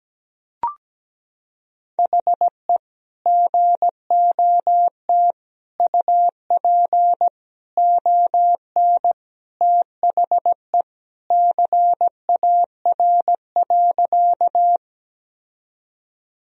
Morse code, heard as a keyed tone, about 17 wpm, 700 Hz. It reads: HE GOT UP ON THE CAR.
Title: Sentences from Top 1000 Words - Review Part 1 - 17wpm